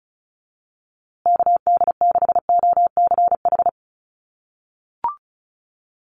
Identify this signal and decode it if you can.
Morse code, heard as a keyed tone, about 34 wpm, 700 Hz. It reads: KD6OCH
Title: USA Call Signs Repeated 3x 35wpm